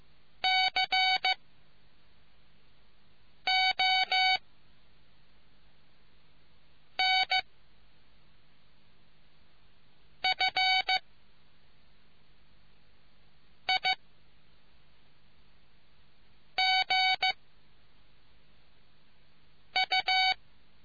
Morse code, heard as a keyed tone, about 15 wpm, spaced as for 3 wpm, 750 Hz. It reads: CONFIGU